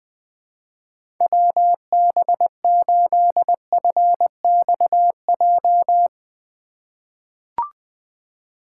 Morse code, heard as a keyed tone, about 20 wpm, 700 Hz. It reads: WB8FXJ
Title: K1USN SST QSO Practice - 20wpm